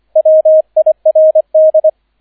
Morse code, heard as a keyed tone, about 25 wpm, 630 Hz, with radio noise. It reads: WIRD